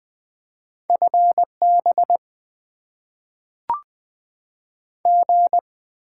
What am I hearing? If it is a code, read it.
Morse code, heard as a keyed tone, about 20 wpm, 700 Hz. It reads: FB G
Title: Top 20 QSO Elements - 6x Character Spacing - 20wpm